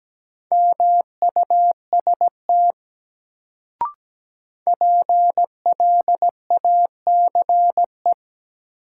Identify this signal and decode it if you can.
Morse code, heard as a keyed tone, about 17 wpm, 700 Hz. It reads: MUST PLACE
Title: Sentences from Top 200 Words - New Word Drill - 17wpm